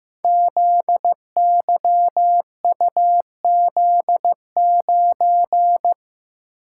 Morse code, heard as a keyed tone, about 15 wpm, 700 Hz. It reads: ZYUZ9
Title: Code Groups - 5 Letter:Numbers - 15wpm